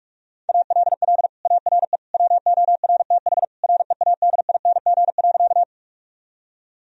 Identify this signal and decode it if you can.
Morse code, heard as a keyed tone, about 45 wpm, 700 Hz. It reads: ALL ARE WORTH READING.
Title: Sentences from Top 1000 Words - Review Part 3 - 45wpm